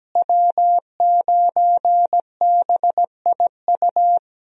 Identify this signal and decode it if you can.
Morse code, heard as a keyed tone, about 17 wpm, 700 Hz. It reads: W9BIU
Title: USA Call Signs - 5 Characters - Repeated 3x - 17wpm